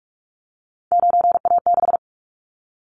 Morse code, heard as a keyed tone, about 44 wpm, 700 Hz. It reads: 9A6
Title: Callsign Prefixes for Top 100 DXCC - 45wpm